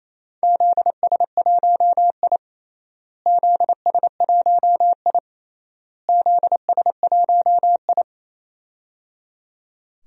Morse code, heard as a keyed tone, about 28 wpm, 700 Hz. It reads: ZS1I ZS1I ZS1I
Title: DXCC-100 Call Signs - 4 Characters - Repeated 3x - 28wpm